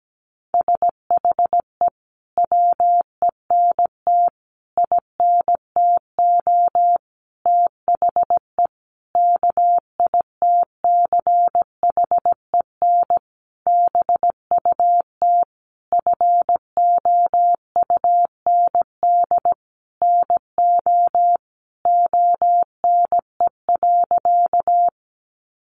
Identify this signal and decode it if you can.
Morse code, heard as a keyed tone, about 17 wpm, 700 Hz. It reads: SHE WENT INTO THE KITCHEN BUT FOUND NO ONE.
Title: Sentences from Top 1400 Words 17wpm